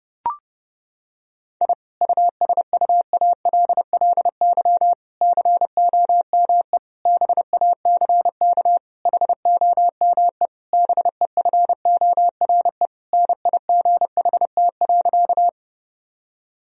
Morse code, heard as a keyed tone, about 32 wpm, 700 Hz. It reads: I USUALLY COME BACK HOME BEFORE NIGHT.
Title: Sentences from Top 1000 Words - Review Part 3 - 30wpm